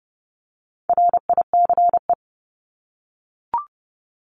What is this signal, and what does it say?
Morse code, heard as a keyed tone, about 30 wpm, 700 Hz. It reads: RICE
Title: Top 1501-1750 Words 30wpm